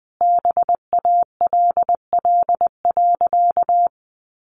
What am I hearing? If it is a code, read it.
Morse code, heard as a keyed tone, about 20 wpm, 700 Hz. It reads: BALL.
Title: Easy Phrases 20wpm